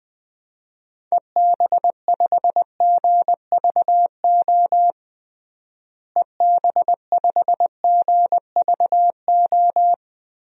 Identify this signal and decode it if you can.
Morse code, heard as a keyed tone, about 20 wpm, 700 Hz. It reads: EB5GVO EB5GVO
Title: DXCC-100 Call Signs - 6 Characters - Repeated 3x - 20wpm